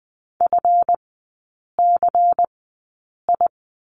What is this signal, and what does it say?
Morse code, heard as a keyed tone, about 20 wpm, spaced as for 8 wpm, 690 Hz. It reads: FCI
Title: 212 Lesson 34e - Number 8 - Three Characters 20wpm